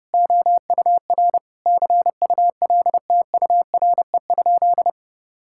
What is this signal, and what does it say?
Morse code, heard as a keyed tone, about 32 wpm, 700 Hz. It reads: OUR CULTURE?
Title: Sentences from Top 1000 Words - Review Part 6 - 30wpm